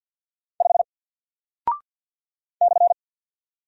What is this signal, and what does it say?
Morse code, heard as a keyed tone, about 48 wpm, 700 Hz. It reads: 5 /
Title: Single Character (Letter:Number:Punctuation:Prosign) Alt 50wpm